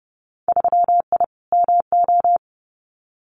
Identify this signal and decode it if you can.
Morse code, heard as a keyed tone, about 30 wpm, 700 Hz. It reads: 3I MO
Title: Winter Field Day Exchanges 30wpm